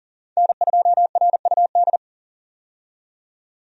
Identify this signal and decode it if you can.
Morse code, heard as a keyed tone, about 40 wpm, 700 Hz. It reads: N2RUD
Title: USA Call Signs - 5 Characters - 40wpm